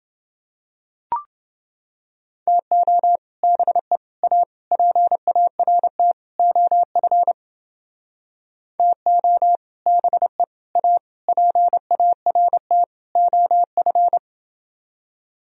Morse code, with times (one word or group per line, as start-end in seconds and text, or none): 2.5-3.2 TO
3.4-4.0 BE
4.2-4.4 A
4.7-6.1 PART
6.4-7.3 OF
8.8-9.6 TO
9.9-10.4 BE
10.8-11.0 A
11.3-12.8 PART
13.2-14.2 OF